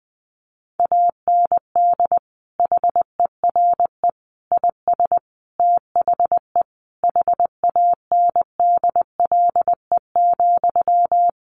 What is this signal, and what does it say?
Morse code, heard as a keyed tone, about 20 wpm, 700 Hz. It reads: AND HERE IS THE HANDLE,
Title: Sentences from Top 1100 Words 20wpm